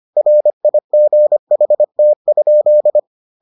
Morse code, heard as a keyed tone, about 25 wpm, 600 Hz. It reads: RIGHT?